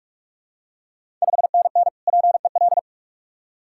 Morse code, heard as a keyed tone, about 46 wpm, 700 Hz. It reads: 5NN PEL
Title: Ontario QSO Party Exchange 45wpm